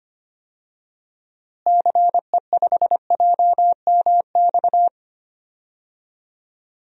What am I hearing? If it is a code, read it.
Morse code, heard as a keyed tone, about 25 wpm, 700 Hz. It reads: CE5JMX
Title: DXCC-100 Call Signs - 6 Characters - Repeated 2x - 25wpm